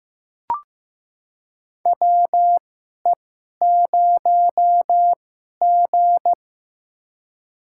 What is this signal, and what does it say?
Morse code, heard as a keyed tone, about 15 wpm, spaced as for 11 wpm, 700 Hz. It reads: WE0G
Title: USA Call Signs - 4 Characters - 2x Character Spacing - 15wpm